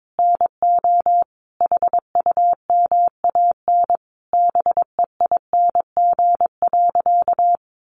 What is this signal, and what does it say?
Morse code, heard as a keyed tone, about 22 wpm, 700 Hz. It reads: NO HUMAN BEING.
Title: Sentences from Top 1500 Words 22wpm